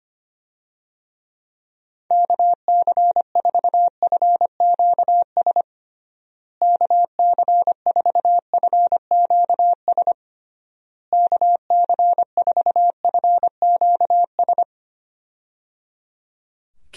Morse code, heard as a keyed tone, about 25 wpm, 700 Hz. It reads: KC4FQS KC4FQS KC4FQS
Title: USA Call Signs - 6 Characters - Repeated 3x - 25wpm